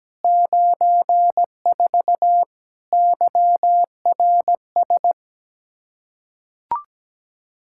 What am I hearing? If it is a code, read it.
Morse code, heard as a keyed tone, about 17 wpm, 700 Hz. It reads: 94 YRS
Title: Practice QSO - Exchange 3 - More Info - 17wpm